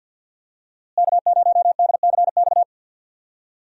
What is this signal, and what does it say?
Morse code, heard as a keyed tone, about 48 wpm, 700 Hz. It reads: K0DCX
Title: USA Call Signs - 5 Characters - Repeated 2x - 50wpm